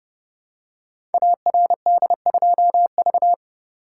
Morse code, heard as a keyed tone, about 30 wpm, 700 Hz. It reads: ARD2V
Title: Code Groups - 5 Letter:Numbers - 30wpm